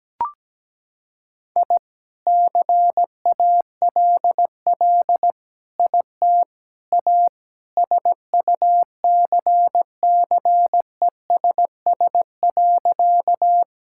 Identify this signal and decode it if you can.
Morse code, heard as a keyed tone, about 17 wpm, 700 Hz. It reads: I CALL IT A SUCCESS.
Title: Sentences from Top 1000 Words - Review Part 6 - 17wpm